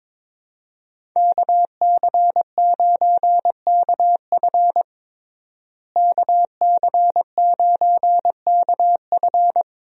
Morse code, heard as a keyed tone, about 22 wpm, 700 Hz. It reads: KC9KF KC9KF
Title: USA Call Signs - 5 Characters - Repeated 2x - 22wpm